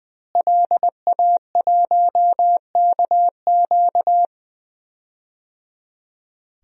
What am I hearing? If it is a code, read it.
Morse code, heard as a keyed tone, about 20 wpm, 700 Hz. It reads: LA1KQ
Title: DXCC-100 Call Signs - 5 Characters - Repeated 3x - 20wpm